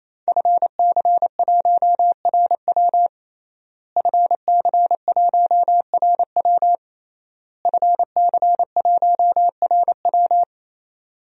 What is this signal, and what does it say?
Morse code, heard as a keyed tone, about 28 wpm, 700 Hz. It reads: FC1RW FC1RW FC1RW
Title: DXCC-100 Call Signs - 5 Characters - Repeated 3x - 28wpm